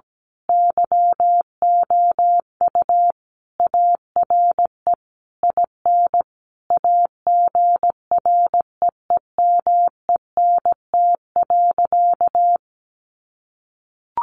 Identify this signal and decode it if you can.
Morse code, heard as a keyed tone, about 17 wpm, 700 Hz. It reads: YOU ARE IN AGREEMENT.
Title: Sentences from Top 1500 Words 17wpm